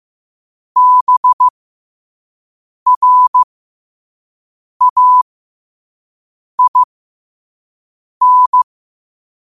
Morse code, heard as a keyed tone, about 15 wpm, spaced as for 5 wpm, 1000 Hz. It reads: BRAIN
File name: qotdfiles / 5wpm